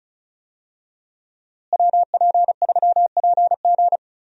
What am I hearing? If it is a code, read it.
Morse code, heard as a keyed tone, about 34 wpm, 700 Hz. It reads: WP3PG
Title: USA Call Signs - 5 Characters - Repeated 3x - 35wpm